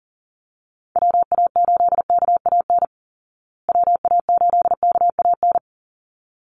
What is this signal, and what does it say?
Morse code, heard as a keyed tone, about 40 wpm, 700 Hz. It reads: WA8KAN WA8KAN
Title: Call Signs Repeated 3x - 40wpm